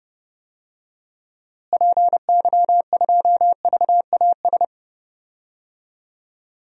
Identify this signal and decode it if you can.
Morse code, heard as a keyed tone, about 32 wpm, 700 Hz. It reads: PY2VAS
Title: DXCC-100 Call Signs - 6 Characters - Repeated 2x - 30wpm